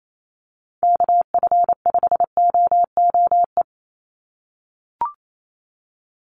Morse code, heard as a keyed tone, about 28 wpm, 700 Hz. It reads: KF5OOE
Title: USA Call Signs 28wpm